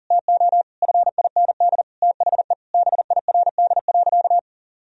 Morse code, heard as a keyed tone, about 40 wpm, 700 Hz. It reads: TO FIND THE BIRD.
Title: Sentences from Top 1000 Words - Review Part 2 - 40wpm